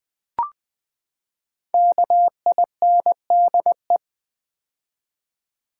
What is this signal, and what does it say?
Morse code, heard as a keyed tone, about 19 wpm, 700 Hz. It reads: KINDE
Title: Build A Word - 5 Letters At A Time - 12-Letter Words - 20wpm